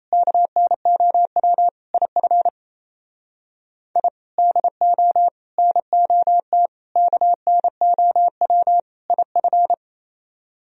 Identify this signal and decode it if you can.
Morse code, heard as a keyed tone, about 37 wpm, 700 Hz. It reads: KNOW IF I DO NOT KNOW IF